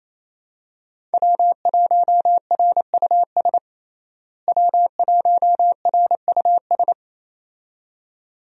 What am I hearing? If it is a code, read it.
Morse code, heard as a keyed tone, about 28 wpm, 700 Hz. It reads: W1RUS W1RUS
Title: USA Call Signs - 5 Characters - Repeated 2x - 28wpm